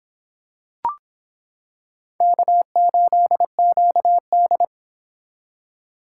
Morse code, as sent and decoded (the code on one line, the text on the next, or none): -.- ---.. --.- -..
K8QD